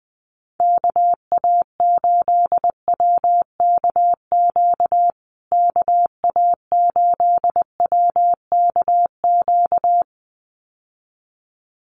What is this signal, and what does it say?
Morse code, heard as a keyed tone, about 20 wpm, 700 Hz. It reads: KA8WKQ KA8WKQ
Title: USA Call Signs Repeated 2x 20wpm